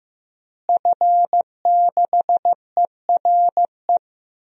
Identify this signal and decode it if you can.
Morse code, heard as a keyed tone, about 15 wpm, 700 Hz. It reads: F6ERE
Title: DXCC-100 Call Signs - 5 Characters - Repeated 3x - 15wpm